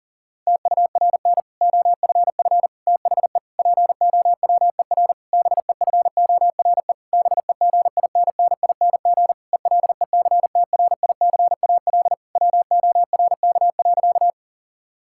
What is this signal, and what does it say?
Morse code, heard as a keyed tone, about 40 wpm, 700 Hz. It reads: TURN OFF THE POWER BEFORE BEGINNING ELECTRICAL WORK.
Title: Sentences from Top 1000 Words - Review Part 6 - 40wpm